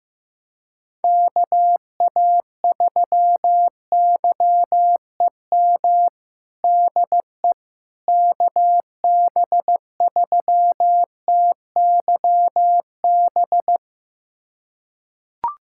Morse code, heard as a keyed tone, about 15 wpm, 700 Hz. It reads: KA3YEM DE KB3TYB
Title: USA Callsign DE Callsign 15wpm